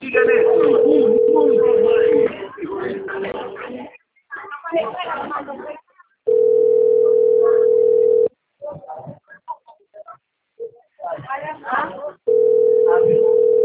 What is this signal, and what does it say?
Ringback tone (North America), 440 + 480 Hz, with speech.
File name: rg-1600-unknown- - - .28627